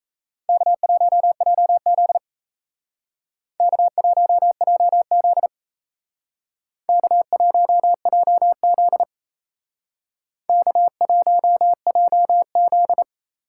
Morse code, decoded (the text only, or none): K1JZ K1JZ K1JZ K1JZ